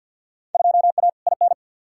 Morse code, heard as a keyed tone, about 51 wpm, 700 Hz. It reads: 2A IN
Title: Field Day Exchanges - 50wpm